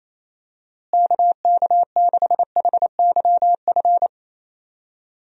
Morse code, heard as a keyed tone, about 28 wpm, 700 Hz. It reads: KK6HYF
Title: USA Call Signs - 6 Characters - Repeated 3x - 28wpm